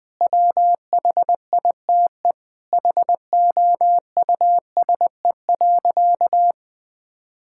Morse code, heard as a keyed tone, about 20 wpm, 700 Hz. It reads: WHITE HOUSE.
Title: Sentences with 4-Word Phrases - Spoken Beginning - 20wpm